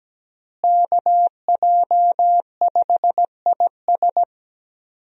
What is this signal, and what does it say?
Morse code, heard as a keyed tone, about 17 wpm, 700 Hz. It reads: KJ5IS
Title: USA Call Signs - 5 Characters - Repeated 2x - 17wpm